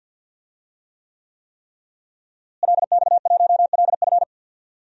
Morse code, heard as a keyed tone, about 52 wpm, 700 Hz. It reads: LX1LF
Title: DXCC-100 Call Signs - 5 Characters - Repeated 3x - 50wpm